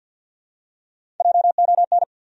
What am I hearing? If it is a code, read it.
Morse code, heard as a keyed tone, about 52 wpm, 700 Hz. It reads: JON
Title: USA Common Names - 50wpm